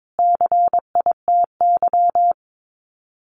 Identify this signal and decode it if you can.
Morse code, heard as a keyed tone, about 22 wpm, 700 Hz. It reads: CITY
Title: Sets of 5 Words Encore2 22wpm